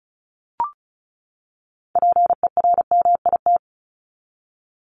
Morse code, heard as a keyed tone, about 34 wpm, 700 Hz. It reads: PERMIT
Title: Top 1751-2000 Words 35wpm